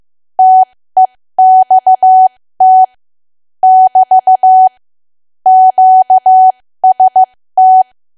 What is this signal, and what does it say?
Morse code, heard as a keyed tone, about 15 wpm, spaced as for 13 wpm, 750 Hz. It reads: TEXT <BT> QST
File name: cwa / W1AW-Level2 / 13WPM